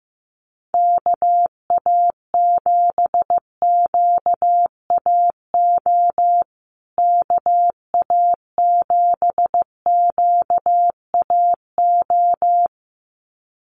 Morse code, heard as a keyed tone, about 15 wpm, 700 Hz. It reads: KA7QAO KA7QAO
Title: USA Call Signs Repeated 2x 15wpm